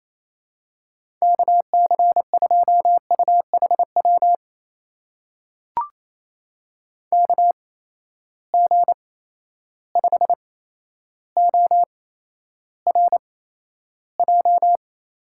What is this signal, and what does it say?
Morse code, heard as a keyed tone, about 28 wpm, spaced as for 12 wpm, 700 Hz. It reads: KC2UHW KG5ORJ
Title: USA Call Signs - 6 Characters - 8x Character Spacing - 28wpm